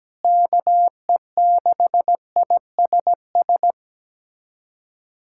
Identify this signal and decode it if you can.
Morse code, heard as a keyed tone, about 17 wpm, 700 Hz. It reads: KE6ISS
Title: USA Callsign DE Callsign 17wpm